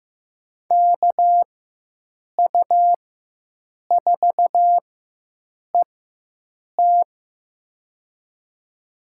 Morse code, heard as a keyed tone, about 15 wpm, spaced as for 7 wpm, 700 Hz. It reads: KU4ET
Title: USA Call Signs - 4x Character Spacing - 15wpm